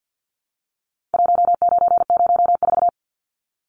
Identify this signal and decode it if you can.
Morse code, heard as a keyed tone, about 50 wpm, 700 Hz. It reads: 1904
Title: Numbers - 4-Digit 50wpm